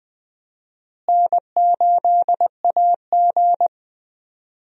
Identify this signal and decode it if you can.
Morse code, heard as a keyed tone, about 20 wpm, 700 Hz. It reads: N8AG
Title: USA Call Signs - 4 Characters - Repeated 2x - 20wpm